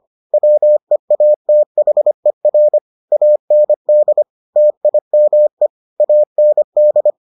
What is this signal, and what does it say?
Morse code, heard as a keyed tone, about 25 wpm, 600 Hz. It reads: WEATHER AND TIME AND